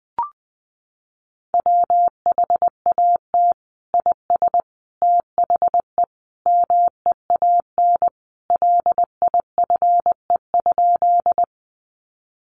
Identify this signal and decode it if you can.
Morse code, heard as a keyed tone, about 20 wpm, 700 Hz. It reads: WHAT IS THE MEAN LIFE?
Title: Sentences from Top 900 Words 20wpm